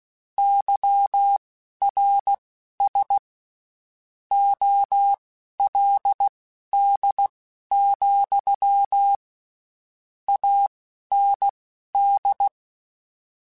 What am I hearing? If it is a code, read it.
Morse code, heard as a keyed tone, about 16 wpm, spaced as for 11 wpm, 800 Hz. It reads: YRS OLD, AND
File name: (spaced) / QSOs Spaced-mp3s / WA7QOX 2 @15 sp